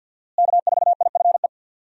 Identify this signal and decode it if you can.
Morse code, heard as a keyed tone, about 52 wpm, 700 Hz. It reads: K4IFE
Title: USA Call Signs - 5 Characters - Repeated 2x - 50wpm